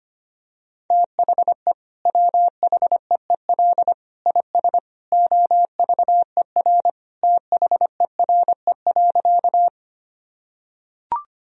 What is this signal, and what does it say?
Morse code, heard as a keyed tone, about 25 wpm, 700 Hz. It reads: THE WHEEL IS OVER THERE.
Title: Sentences from Top 1000 Words - Review Part 3 - 25wpm